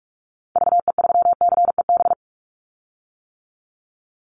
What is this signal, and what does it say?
Morse code, heard as a keyed tone, about 44 wpm, 700 Hz. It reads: VE3CEB
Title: Call Signs - 45wpm